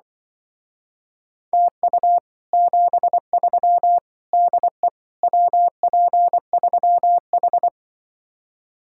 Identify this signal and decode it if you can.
Morse code, heard as a keyed tone, about 24 wpm, 700 Hz. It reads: TU 73 DE WP3H
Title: Speed-Racing Common QSO Phrases - 20wpm